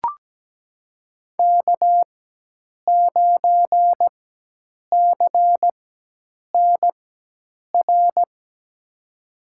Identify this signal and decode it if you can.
Morse code, heard as a keyed tone, about 17 wpm, spaced as for 8 wpm, 700 Hz. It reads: K9CNR
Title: USA Call Signs - 5 Characters - 4x Character Spacing - 17wpm